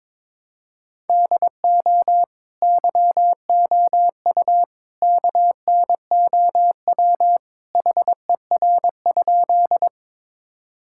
Morse code, heard as a keyed tone, about 22 wpm, 700 Hz. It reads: DO YOU KNOW HER?
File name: practice / 22wpm / Sentences with 3-Word Phrases - 22wpm